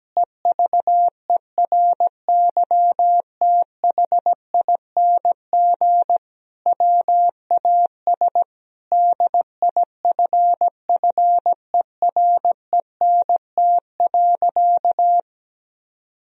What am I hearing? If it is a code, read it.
Morse code, heard as a keyed tone, about 17 wpm, 700 Hz. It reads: EVERYTHING WAS DIFFERENT.
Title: Sentences from Top 1000 Words - Review Part 3 - 17wpm